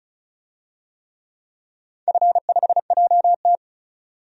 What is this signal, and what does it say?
Morse code, heard as a keyed tone, about 36 wpm, 700 Hz. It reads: F5JT
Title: DXCC-100 Call Signs - 4 Characters - 35wpm